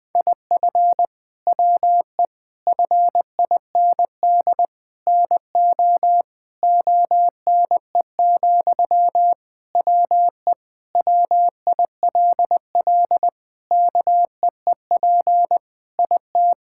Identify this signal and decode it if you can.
Morse code, heard as a keyed tone, about 20 wpm, 700 Hz. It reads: IF WE FIND NO ONE, WE WILL KEEP IT
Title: Sentences from Top 1600 Words 20wpm